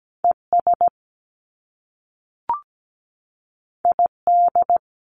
Morse code, heard as a keyed tone, about 17 wpm, 700 Hz. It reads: ES ID
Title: Top Two Letter Combinations in Words 17wpm